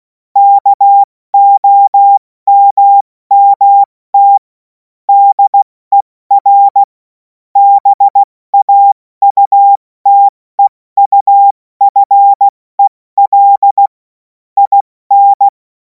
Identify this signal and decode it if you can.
Morse code, heard as a keyed tone, about 16 wpm, spaced as for 14 wpm, 800 Hz. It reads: KOMMT DER BAUTEUFEL IN
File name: Morsekurs / Witze / Witz 03. März 2026 70 F80